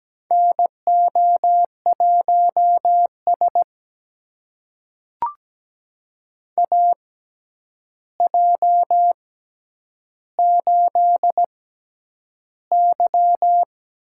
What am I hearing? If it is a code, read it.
Morse code, heard as a keyed tone, about 17 wpm, spaced as for 9 wpm, 700 Hz. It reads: NO1S AJ8Y
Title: USA Call Signs - 4 Characters - 6x Character Spacing - 17wpm